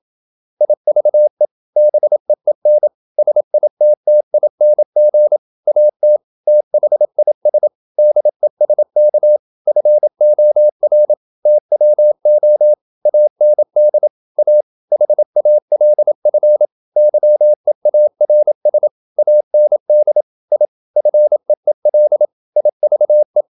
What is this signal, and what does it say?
Morse code, heard as a keyed tone, about 27 wpm, 600 Hz. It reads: IVE BEEN SITTING AT THIS DESK FOR TWO AND A HALF YEARS AND I FEEL IVE
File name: Never 0027wpm